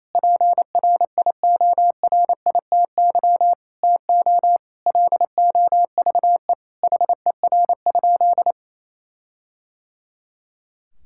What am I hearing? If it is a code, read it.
Morse code, heard as a keyed tone, about 28 wpm, 700 Hz. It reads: PRIORITY TO LOVE HER?
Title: Sentences from Top 1600 Words 28wpm